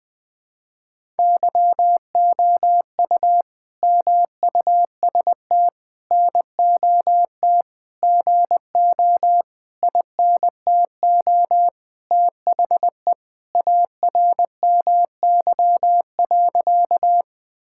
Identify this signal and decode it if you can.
Morse code, heard as a keyed tone, about 20 wpm, 700 Hz. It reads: YOU MUST NOT GO INTO THE ARMY.
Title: Sentences from Top 1000 Words - Review Part 5 - 20wpm